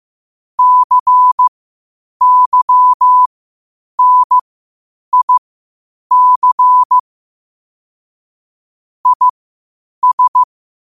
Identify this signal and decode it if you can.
Morse code, heard as a keyed tone, about 15 wpm, spaced as for 8 wpm, 1000 Hz. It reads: CYNIC IS
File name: qotdfiles / 8wpm